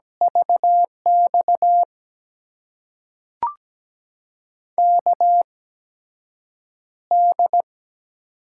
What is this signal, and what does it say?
Morse code, heard as a keyed tone, about 17 wpm, spaced as for 5 wpm, 700 Hz. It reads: VXKD
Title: USA Call Signs - 6 Characters - 8x Character Spacing - 17wpm